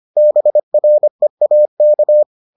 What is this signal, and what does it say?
Morse code, heard as a keyed tone, about 25 wpm, 600 Hz. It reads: BREAK